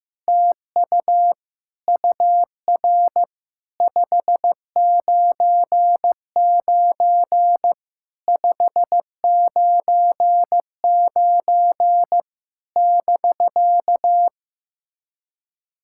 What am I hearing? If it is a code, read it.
Morse code, heard as a keyed tone, about 15 wpm, 700 Hz. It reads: TU UR 599 599 <BK>
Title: SOTA QSO Practice 15wpm